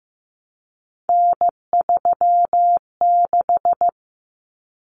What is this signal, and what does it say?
Morse code, heard as a keyed tone, about 15 wpm, 700 Hz. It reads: N36
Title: Common Cut Numbers - 3-Digit 15wpm